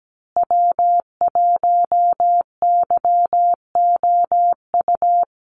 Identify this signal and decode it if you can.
Morse code, heard as a keyed tone, about 17 wpm, 700 Hz. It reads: W1YOU
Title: USA Call Signs Repeated 3x 17wpm